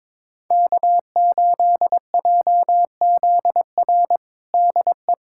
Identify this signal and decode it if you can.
Morse code, heard as a keyed tone, about 21 wpm, 700 Hz. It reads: K8JZR DE
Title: USA Callsign DE Callsign - 5 Characters - 22wpm